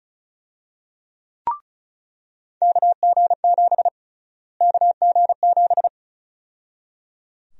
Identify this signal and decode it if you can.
Morse code, heard as a keyed tone, about 34 wpm, 700 Hz. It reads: KG7 KG7
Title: USA Call Sign Prefixes - Repeated 2x - 35wpm